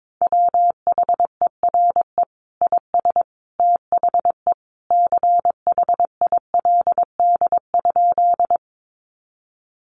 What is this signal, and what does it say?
Morse code, heard as a keyed tone, about 22 wpm, 700 Hz. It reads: WHERE IS THE CHILD?
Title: Sentences from Top 800 Words 22wpm